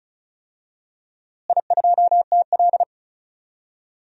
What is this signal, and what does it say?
Morse code, heard as a keyed tone, about 36 wpm, 700 Hz. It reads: I2TL
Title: DXCC-100 Call Signs - 4 Characters - Repeated 3x - 35wpm